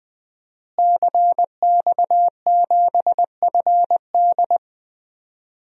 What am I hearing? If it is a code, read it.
Morse code, heard as a keyed tone, about 20 wpm, 700 Hz. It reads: CX7FD
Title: DXCC-100 Call Signs - 5 Characters - Repeated 2x - 20wpm